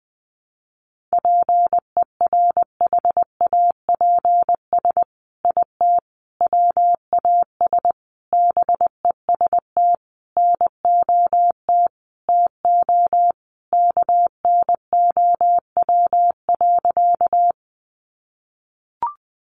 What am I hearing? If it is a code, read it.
Morse code, heard as a keyed tone, about 20 wpm, 700 Hz. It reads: PERHAPS IT WAS BEST NOT TO KNOW.
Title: Sentences from Top 600 Words 20wpm